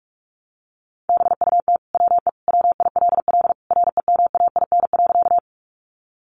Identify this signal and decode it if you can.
Morse code, heard as a keyed tone, about 44 wpm, 700 Hz. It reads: BUT WE WILL REMAIN.
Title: Sentences from Top 1200 Words 45wpm